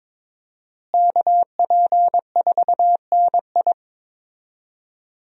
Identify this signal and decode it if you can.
Morse code, heard as a keyed tone, about 22 wpm, 700 Hz. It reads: KP4NI
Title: USA Call Signs - 5 Characters - Repeated 3x - 22wpm